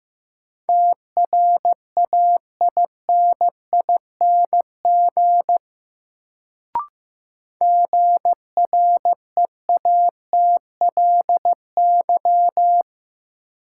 Morse code, heard as a keyed tone, about 15 wpm, 700 Hz. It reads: TRAINING GREATLY
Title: Sentences from Top 1400 Words - New Word Drill Alt - 15wpm